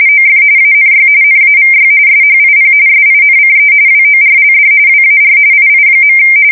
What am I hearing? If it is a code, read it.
RTTY, about 45.45 baud. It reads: OF ANY NAVY SHIP SINCE WORLD WAR II.
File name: fsk170 / USS-New-Jersey